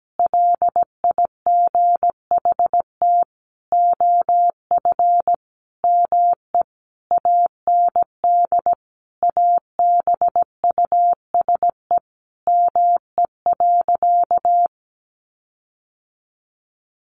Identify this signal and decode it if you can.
Morse code, heard as a keyed tone, about 17 wpm, 700 Hz. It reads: LIGHT OF ME AND ABUSE ME.
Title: Sentences from Top 1300 Words 17wpm